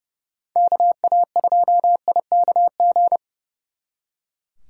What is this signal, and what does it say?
Morse code, heard as a keyed tone, about 35 wpm, 700 Hz. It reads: KA2IKG
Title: Speed Racing - USA 2x3 Call Signs 30wpm